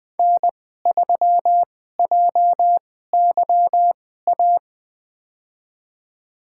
Morse code, heard as a keyed tone, about 20 wpm, spaced as for 14 wpm, 700 Hz. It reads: N3JYA
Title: USA Call Signs - 2x Character Spacing - 20wpm